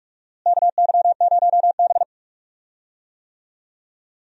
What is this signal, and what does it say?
Morse code, heard as a keyed tone, about 42 wpm, 700 Hz. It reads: KY0B
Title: USA Call Signs - 4 Characters - Repeated 3x - 45wpm